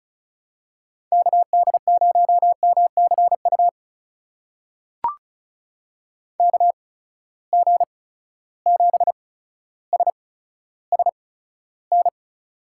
Morse code, heard as a keyed tone, about 36 wpm, spaced as for 15 wpm, 700 Hz. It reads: KD0MCU KG7SSN
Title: USA Call Signs - 6 Characters - 8x Character Spacing - 35wpm